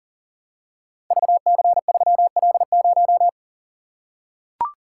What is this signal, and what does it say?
Morse code, heard as a keyed tone, about 38 wpm, 700 Hz. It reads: VC3L0
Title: Code Groups - 5 Letter:Numbers - 40wpm